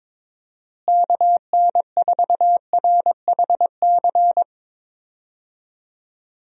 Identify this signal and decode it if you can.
Morse code, heard as a keyed tone, about 22 wpm, 700 Hz. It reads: KN4RHC
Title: USA Call Signs - 6 Characters - Repeated 3x - 22wpm